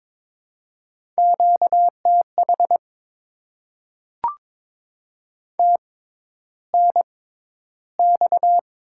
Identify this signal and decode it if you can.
Morse code, heard as a keyed tone, about 22 wpm, spaced as for 11 wpm, 690 Hz. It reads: QTH TNX
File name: Top 20 QSO Elements - 6x Character Spacing - 22wpm